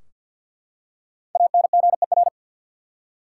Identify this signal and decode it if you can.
Morse code, heard as a keyed tone, about 51 wpm, 700 Hz. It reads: ANGER
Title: Five Letter Words - 50wpm